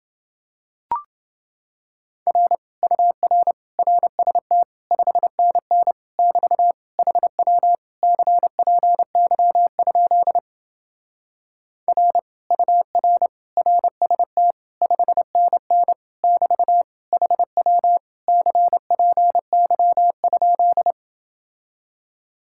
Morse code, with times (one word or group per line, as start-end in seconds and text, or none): 2.3-2.6 R
2.8-3.5 UR
3.8-4.6 RST
4.9-5.9 5NN
6.2-6.7 <BT>
7.0-7.8 HW
8.0-10.4 CPY?
11.9-12.2 R
12.5-13.3 UR
13.6-14.5 RST
14.8-15.9 5NN
16.2-16.8 <BT>
17.1-18.0 HW
18.3-20.9 CPY?